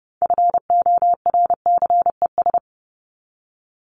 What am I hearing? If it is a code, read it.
Morse code, heard as a keyed tone, about 30 wpm, 700 Hz. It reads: FORCES
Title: ES:S Suffix - Top 100 Words with Base Words 30wpm